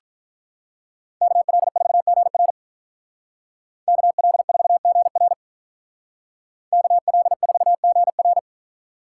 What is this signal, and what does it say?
Morse code, heard as a keyed tone, about 43 wpm, 700 Hz. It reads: KL4GR KL4GR KL4GR
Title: Speed Racing - USA 2x2 Call Signs 35wpm